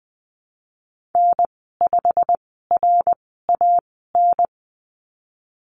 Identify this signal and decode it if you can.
Morse code, heard as a keyed tone, about 20 wpm, spaced as for 15 wpm, 700 Hz. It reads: N5RAN
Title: 167 Lesson 27g - Number 7 - Call Signs 20wpm